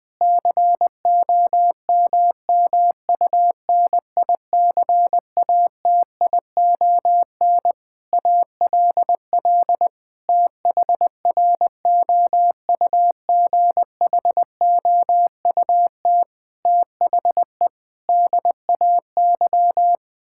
Morse code, heard as a keyed tone, about 20 wpm, 700 Hz. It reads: COMMUNICATION ALL THROUGHOUT THE DAY
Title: Sentences from Top 1000 Words - Review Part 6 - 20wpm